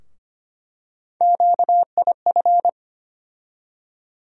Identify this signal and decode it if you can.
Morse code, heard as a keyed tone, about 25 wpm, 700 Hz. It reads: QIF
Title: USA Call Sign Postfixes - 3 Letters 25wpm